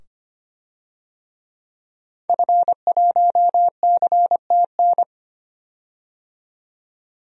Morse code, heard as a keyed tone, about 25 wpm, 700 Hz. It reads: F1CTN
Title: DXCC-100 Call Signs - 5 Characters - 25wpm